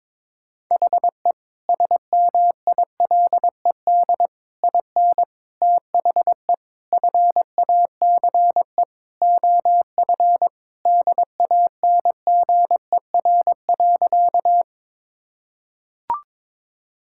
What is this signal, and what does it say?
Morse code, heard as a keyed tone, about 22 wpm, 700 Hz. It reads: HE SMILED IN THE FACE OF DANGER.
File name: practice / 22wpm / Sentences with 4-Word Phrases - 22wpm